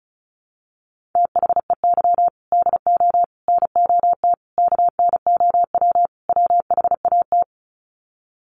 Morse code, heard as a keyed tone, about 34 wpm, 700 Hz. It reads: THEY DO NOT KNOW WHAT
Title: Sets of 5 Words Encore 35wpm